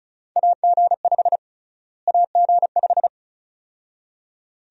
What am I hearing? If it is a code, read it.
Morse code, heard as a keyed tone, about 36 wpm, 700 Hz. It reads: AG5 AG5
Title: USA Call Sign Prefixes - Repeated 2x - 35wpm